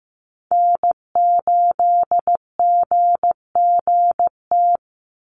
Morse code, heard as a keyed tone, about 15 wpm, 700 Hz. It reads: N8GGT
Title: USA Call Signs 15wpm